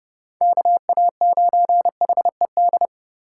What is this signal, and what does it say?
Morse code, heard as a keyed tone, about 29 wpm, 700 Hz. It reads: KA9HED